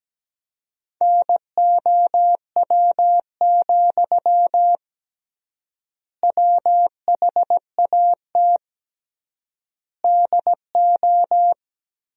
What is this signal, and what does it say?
Morse code, heard as a keyed tone, about 17 wpm, 700 Hz. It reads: NOW, WHAT DO
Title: 3x Word-Spacing - Sentences from Top 200 Words - 17wpm